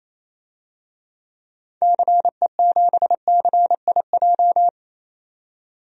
Morse code, heard as a keyed tone, about 28 wpm, 700 Hz. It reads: CE7CIJ
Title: DXCC-100 Call Signs - 6 Characters - 28wpm